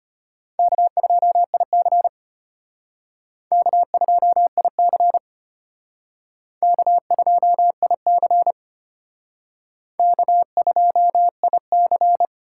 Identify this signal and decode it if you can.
Morse code, heard as a keyed tone, about 33 wpm, 700 Hz. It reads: K2IC K2IC K2IC K2IC